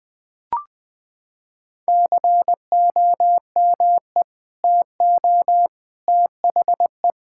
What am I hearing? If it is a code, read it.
Morse code, heard as a keyed tone, about 20 wpm, 700 Hz. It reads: COME TO THE
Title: Sentences with 2-Word Phrases - 20wpm